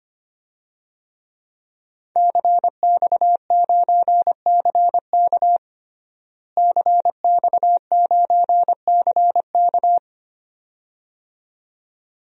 Morse code, heard as a keyed tone, about 25 wpm, 700 Hz. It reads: CX9CK CX9CK
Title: DXCC-100 Call Signs - 5 Characters - Repeated 2x - 25wpm